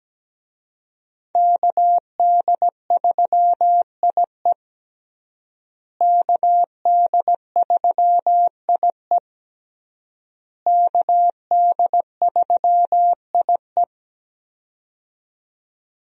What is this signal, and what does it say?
Morse code, heard as a keyed tone, about 17 wpm, 700 Hz. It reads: KD3IE KD3IE KD3IE
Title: USA Call Signs - 5 Characters - Repeated 3x - 17wpm